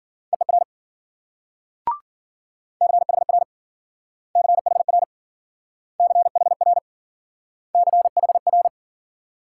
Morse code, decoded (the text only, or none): EER CHR CHR CHR CHR